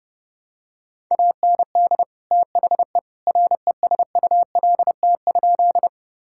Morse code, heard as a keyed tone, about 30 wpm, 700 Hz. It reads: AND THE RESULT?
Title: Sentences from Top 1000 Words - Review Part 5 - 30wpm